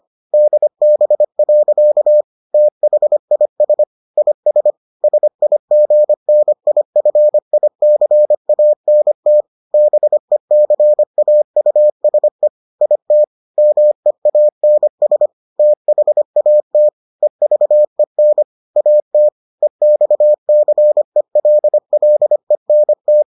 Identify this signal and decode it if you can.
Morse code, heard as a keyed tone, about 25 wpm, 600 Hz. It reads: DB. THIS IS SIGNIFICANT BECAUSE IT MEANS THAT EVEN AT EXCELLENT